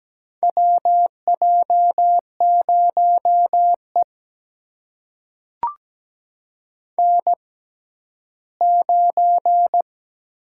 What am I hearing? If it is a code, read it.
Morse code, heard as a keyed tone, about 17 wpm, spaced as for 12 wpm, 700 Hz. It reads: WJ0E N9
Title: USA Call Signs - 4 Characters - 6x Character Spacing - 17wpm